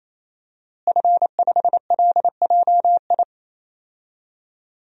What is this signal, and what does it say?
Morse code, heard as a keyed tone, about 28 wpm, 700 Hz. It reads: F5LJI
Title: DXCC-100 Call Signs - 5 Characters - Repeated 3x - 28wpm